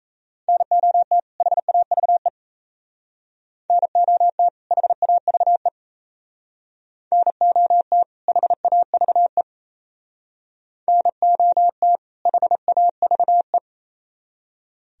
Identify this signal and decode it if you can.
Morse code, heard as a keyed tone, about 35 wpm, 700 Hz. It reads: NOT HAVE NOT HAVE NOT HAVE NOT HAVE